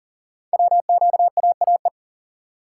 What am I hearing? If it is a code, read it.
Morse code, heard as a keyed tone, about 41 wpm, 700 Hz. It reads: WQAAE
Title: Code Groups - 5 Letter:Numbers - 40wpm